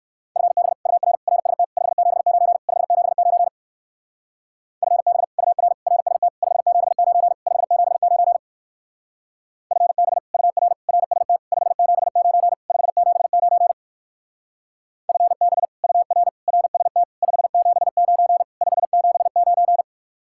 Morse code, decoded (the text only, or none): FB UR RST 579 579 FB UR RST 579 579 FB UR RST 579 579 FB UR RST 579 579